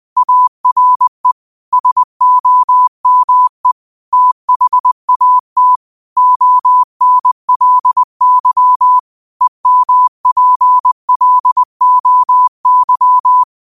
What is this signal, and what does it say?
Morse code, heard as a keyed tone, about 20 wpm, 1000 Hz. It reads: ARE SOME THAT ONLY EMPLOY
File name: qotdfiles / 20wpm